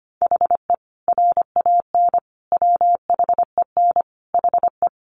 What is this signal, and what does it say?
Morse code, heard as a keyed tone, about 25 wpm, 700 Hz. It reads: HE RAN WHEN HE
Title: Sentences from Top 500 Words 25wpm